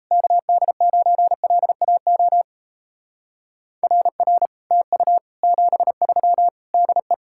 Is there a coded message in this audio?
Yes: Morse code from 0.1 to 7.1 s, heard as a keyed tone, about 36 wpm, 700 Hz. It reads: KD9LAO RR TU 73 DE